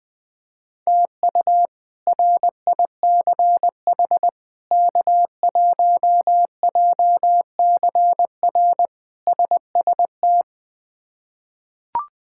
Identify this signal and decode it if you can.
Morse code, heard as a keyed tone, about 20 wpm, 700 Hz. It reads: TU RICH K1JCR SST
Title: K1USN SST QSO Practice - 20wpm